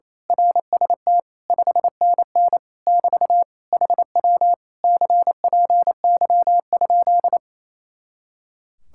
Morse code, heard as a keyed tone, about 26 wpm, 700 Hz. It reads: RST 5NN <BT> HW CPY?
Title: Speed-Racing Common QSO Phrases Encore - 28wpm